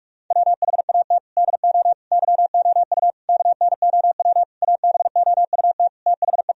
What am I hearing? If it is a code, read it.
Morse code, heard as a keyed tone, about 41 wpm, 700 Hz. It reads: WHAT DO YOU KNOW ABOUT THE
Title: Sentences from Top 1000 Words - Review Part 4 - 45wpm